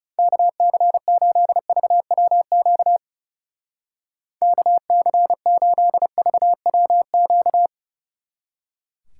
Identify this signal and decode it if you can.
Morse code, heard as a keyed tone, about 38 wpm, 700 Hz. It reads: KC8VWQ KC8VWQ